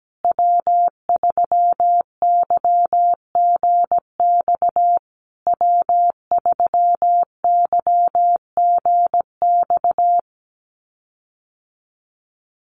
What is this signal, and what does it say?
Morse code, heard as a keyed tone, about 17 wpm, 700 Hz. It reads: W3YGX W3YGX
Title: USA Call Signs Repeated 3x 17wpm